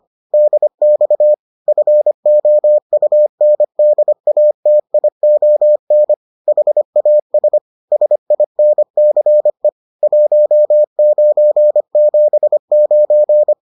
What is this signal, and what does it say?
Morse code, heard as a keyed tone, about 25 wpm, 600 Hz. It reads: DX FOUNDATION HAS SINCE 1979